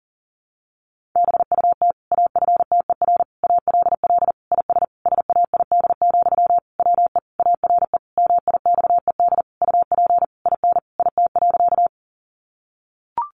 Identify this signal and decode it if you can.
Morse code, heard as a keyed tone, about 40 wpm, 700 Hz. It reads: BUT AFTER ALL IS SAID, WE ARE MIXED UP IN IT.
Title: Sentences from Top 1500 Words 40wpm